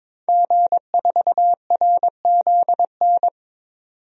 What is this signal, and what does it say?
Morse code, heard as a keyed tone, about 21 wpm, 700 Hz. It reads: G4RZN
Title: DXCC-100 Call Signs - 5 Characters - 22wpm